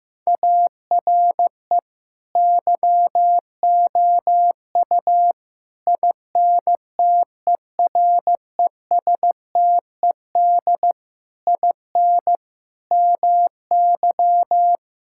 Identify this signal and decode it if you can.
Morse code, heard as a keyed tone, about 15 wpm, 700 Hz. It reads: ARE YOU INTERESTED IN MY
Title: Sentences from Top 1000 Words - Review Part 6 - 15wpm